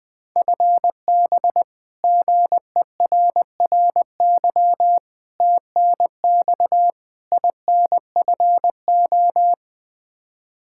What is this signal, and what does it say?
Morse code, heard as a keyed tone, about 20 wpm, 700 Hz. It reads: FB GERRY TNX INFO
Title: Practice QSO - Exchange 3 - More Info - 20wpm